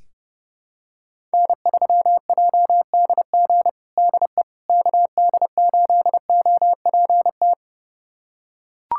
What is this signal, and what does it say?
Morse code, heard as a keyed tone, about 30 wpm, 700 Hz. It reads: N3JDG DE KD8OPT
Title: USA Callsign DE Callsign 30wpm